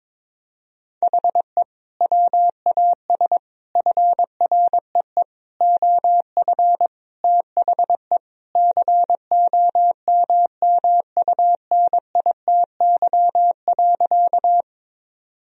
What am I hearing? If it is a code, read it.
Morse code, heard as a keyed tone, about 22 wpm, 700 Hz. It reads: HE WAS FREE OF THE COMMUNITY.
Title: Sentences from Top 1000 Words - Review Part 4 - 22wpm